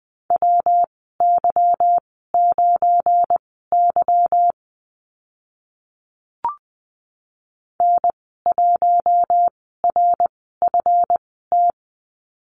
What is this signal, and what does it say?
Morse code, heard as a keyed tone, about 20 wpm, spaced as for 14 wpm, 700 Hz. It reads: WY9Y N1RFT
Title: 193 Lesson 31g - Letter v - Call Signs 20wpm